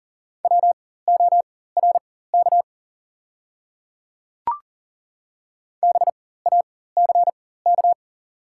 Morse code, heard as a keyed tone, about 40 wpm, spaced as for 19 wpm, 690 Hz. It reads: WORK BACK
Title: 4x Character Spacing - Top 200 Words - 40wpm